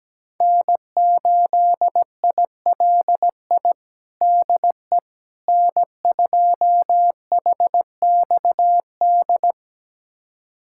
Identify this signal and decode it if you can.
Morse code, heard as a keyed tone, about 17 wpm, 700 Hz. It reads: N8ILI DE N2HXD
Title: USA Callsign DE Callsign - 5 Characters - 17wpm